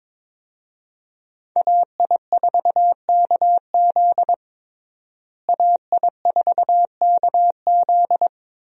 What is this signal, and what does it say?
Morse code, heard as a keyed tone, about 22 wpm, 690 Hz. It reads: AI4KZ AI4KZ
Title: USA Call Signs - 5 Characters - Repeated 3x - 22wpm